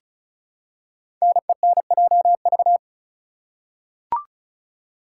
Morse code, heard as a keyed tone, about 36 wpm, 690 Hz. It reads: NENJV
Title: Code Groups - 5 Letters - 35wpm